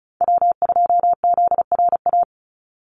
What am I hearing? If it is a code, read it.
Morse code, heard as a keyed tone, about 34 wpm, 700 Hz. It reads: W2ZRA
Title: Call Signs - 35wpm